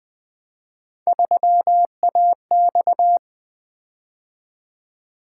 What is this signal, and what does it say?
Morse code, heard as a keyed tone, about 20 wpm, 700 Hz. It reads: 3AX
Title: Three Letter-Number Combinations - 20wpm